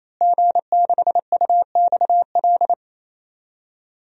Morse code, heard as a keyed tone, about 28 wpm, 700 Hz. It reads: G6UXL
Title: DXCC-100 Call Signs - 5 Characters - Repeated 3x - 28wpm